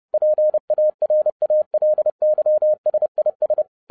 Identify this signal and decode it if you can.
Morse code, heard as a keyed tone, about 30 wpm, 610 Hz. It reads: PARALYSIS
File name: 2011-07-mp3 / n60 BACK FROM A BIG ACCIDENT 30wpm